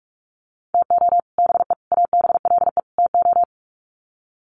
Morse code, heard as a keyed tone, about 45 wpm, 700 Hz. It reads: TO BE ABLE TO
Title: Sets of 4 Words 45wpm